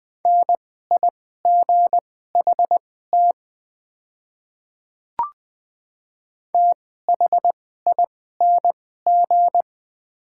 Morse code, heard as a keyed tone, about 20 wpm, spaced as for 15 wpm, 700 Hz. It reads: NIGHT THING
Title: 2x Character Spacing - Top 300 Words - 20wpm